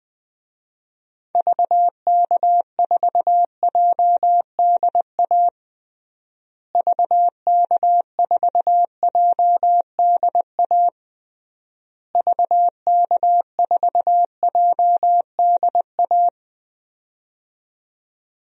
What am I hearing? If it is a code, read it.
Morse code, heard as a keyed tone, about 20 wpm, 700 Hz. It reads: VK4JDA VK4JDA VK4JDA
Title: DXCC-100 Call Signs - 6 Characters - Repeated 3x - 20wpm